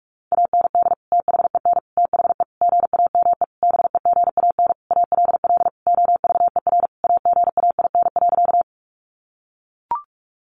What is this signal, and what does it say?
Morse code, heard as a keyed tone, about 44 wpm, 700 Hz. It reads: AND THEN THE GAME BEGAN ALL OVER AGAIN.
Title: Sentences from Top 500 Words 45wpm